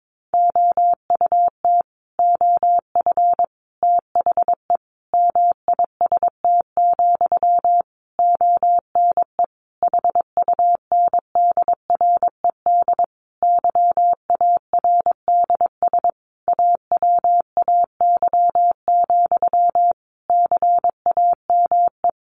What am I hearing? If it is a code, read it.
Morse code, heard as a keyed tone, about 22 wpm, 700 Hz. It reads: OUT OF THE MIST, ONE HUNDRED YARDS AWAY, CAME
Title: A Sound Of Thunder 22wpm